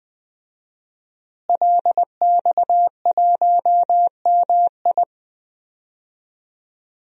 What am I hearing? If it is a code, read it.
Morse code, heard as a keyed tone, about 21 wpm, 700 Hz. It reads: LX1MI
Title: DXCC-100 Call Signs - 5 Characters - Repeated 2x - 20wpm